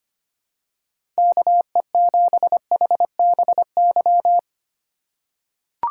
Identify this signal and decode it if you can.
Morse code, heard as a keyed tone, about 25 wpm, 700 Hz. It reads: KE7HBY
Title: USA Call Signs - 2x Character Spacing - 25wpm